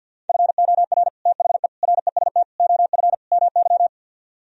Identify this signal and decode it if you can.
Morse code, heard as a keyed tone, about 64 wpm, 700 Hz. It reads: FOR THE REST OF MY